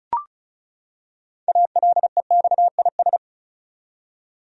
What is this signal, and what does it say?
Morse code, heard as a keyed tone, about 36 wpm, 700 Hz. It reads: ALEXIS
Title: USA Common Names - 35wpm